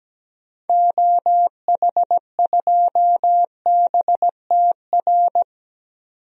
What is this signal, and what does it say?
Morse code, heard as a keyed tone, about 17 wpm, 700 Hz. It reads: OH2BTR
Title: DXCC-100 Call Signs - 6 Characters - 17wpm